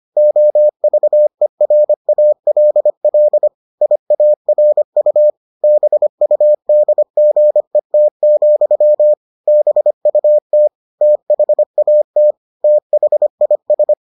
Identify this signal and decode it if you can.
Morse code, heard as a keyed tone, about 25 wpm, 600 Hz. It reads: OVERALL IARU BUDGET, BUT THAT THIS